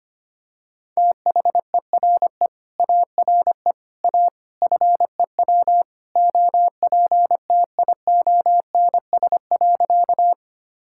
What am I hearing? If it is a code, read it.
Morse code, heard as a keyed tone, about 25 wpm, 700 Hz. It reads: THERE ARE A FEW OPTIONS.
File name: practice / 25wpm / Sentences with 3-Word Phrases - 25wpm